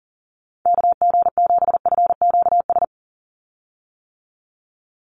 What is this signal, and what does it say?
Morse code, heard as a keyed tone, about 40 wpm, 700 Hz. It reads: KG7FQS
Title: USA Call Signs 40wpm